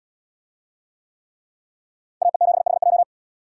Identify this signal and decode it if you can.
Morse code, heard as a keyed tone, about 71 wpm, 700 Hz. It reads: AE7HJ